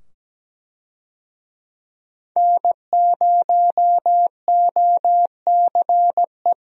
Morse code, heard as a keyed tone, about 17 wpm, 700 Hz. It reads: N0OCE
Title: USA Call Signs - 5 Characters - 17wpm